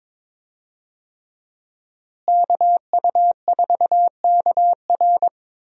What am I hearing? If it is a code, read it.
Morse code, heard as a keyed tone, about 22 wpm, 700 Hz. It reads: KU4KR
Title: USA Call Signs - 5 Characters - Repeated 3x - 22wpm